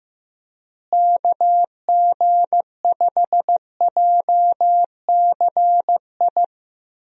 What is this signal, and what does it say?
Morse code, heard as a keyed tone, about 15 wpm, 700 Hz. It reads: KG5JCI
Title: USA Call Signs - 6 Characters - Repeated 3x - 15wpm